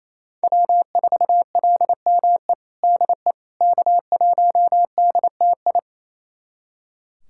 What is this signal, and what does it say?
Morse code, heard as a keyed tone, about 28 wpm, 700 Hz. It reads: W4LME DE K1DTI
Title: USA Callsign DE Callsign - 5 Characters - 28wpm